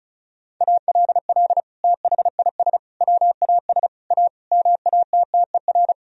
Morse code, heard as a keyed tone, about 36 wpm, 700 Hz. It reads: ALL THIS WAS A MATTER
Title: Sentences from Top 1000 Words - Review Part 4 - 35wpm